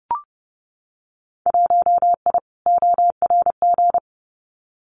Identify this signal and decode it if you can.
Morse code, heard as a keyed tone, about 30 wpm, 700 Hz. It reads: 1I ORG
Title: Winter Field Day Exchanges 30wpm